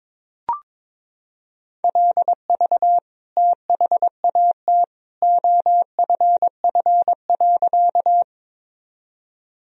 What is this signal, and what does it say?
Morse code, heard as a keyed tone, about 22 wpm, 700 Hz. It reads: LV THAT OFF.
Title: Sentences from Top 400 Words and Abbreviations 22wpm